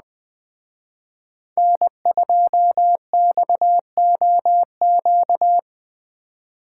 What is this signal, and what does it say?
Morse code, heard as a keyed tone, about 20 wpm, 700 Hz. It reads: N2XOQ